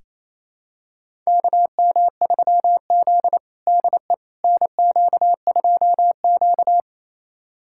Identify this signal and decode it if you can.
Morse code, heard as a keyed tone, about 27 wpm, 700 Hz. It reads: KM3Z DE NQ2Q
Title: USA Callsign DE Callsign - 4 Characters - 28wpm